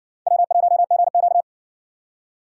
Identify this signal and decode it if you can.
Morse code, heard as a keyed tone, about 60 wpm, 700 Hz. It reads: W1GQ